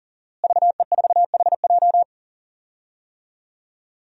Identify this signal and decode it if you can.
Morse code, heard as a keyed tone, about 40 wpm, 700 Hz. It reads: VE4HJ
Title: DXCC-100 Call Signs - 5 Characters - Repeated 2x - 40wpm